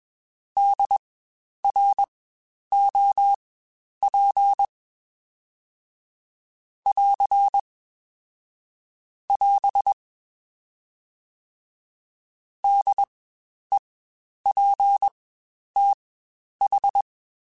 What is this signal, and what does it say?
Morse code, heard as a keyed tone, about 21 wpm, spaced as for 10 wpm, 800 Hz. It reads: DROP <AR> <AS> DEPTH
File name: Spaced Alphas / Spaced Alphas mp3s / Ds sp @20